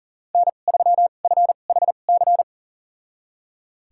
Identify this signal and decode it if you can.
Morse code, heard as a keyed tone, about 40 wpm, spaced as for 29 wpm, 700 Hz. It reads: N3FHC
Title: USA Call Signs - 5 Characters - 2x Character Spacing - 40wpm